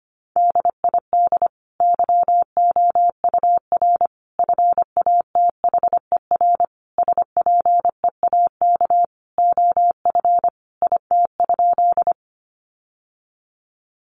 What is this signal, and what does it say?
Morse code, heard as a keyed tone, about 25 wpm, 700 Hz. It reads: DID YOUR FATHER SPEAK OF IT?
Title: Sentences from Top 1100 Words 25wpm